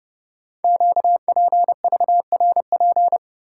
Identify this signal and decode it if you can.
Morse code, heard as a keyed tone, about 30 wpm, 690 Hz. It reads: QPVRP
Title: Code Groups - 5 Letters - 30wpm